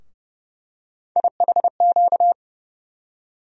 Morse code, heard as a keyed tone, about 30 wpm, 700 Hz. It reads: IHQ
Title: USA Call Sign Postfixes - 3 Letters 30wpm